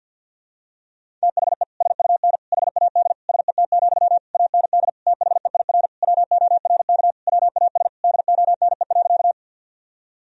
Morse code, heard as a keyed tone, about 45 wpm, 700 Hz. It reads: THE SUN HAD SET, AND THEIR WORK WAS DONE.
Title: Sentences from Top 1000 Words - Review Part 1 - 50wpm